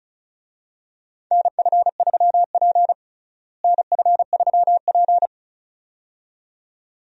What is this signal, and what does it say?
Morse code, heard as a keyed tone, about 36 wpm, 700 Hz. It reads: NF3P NF3P
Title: USA Call Signs - 4 Characters - Repeated 2x - 35wpm